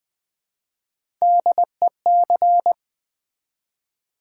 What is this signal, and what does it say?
Morse code, heard as a keyed tone, about 20 wpm, 700 Hz. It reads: DEC
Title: Build A Word - 1 Letter At A Time - 4-Letter Words - 20wpm